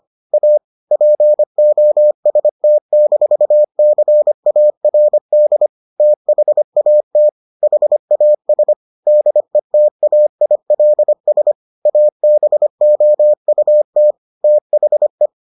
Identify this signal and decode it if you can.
Morse code, heard as a keyed tone, about 25 wpm, 600 Hz. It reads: A POST-CARD THAT HAS DETAILS ABOUT THE